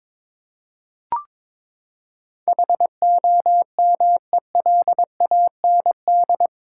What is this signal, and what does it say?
Morse code, heard as a keyed tone, about 22 wpm, 700 Hz. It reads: HOMELAND
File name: practice / 22wpm / Eight Letter Words - 22wpm